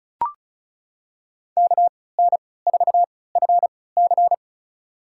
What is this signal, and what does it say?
Morse code, heard as a keyed tone, about 36 wpm, spaced as for 20 wpm, 700 Hz. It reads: KN4FC
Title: USA Call Signs - 5 Characters - 3x Character Spacing - 35wpm